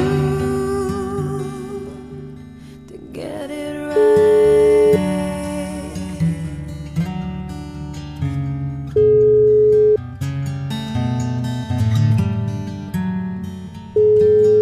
Ringback tone (Europe, 425 Hz).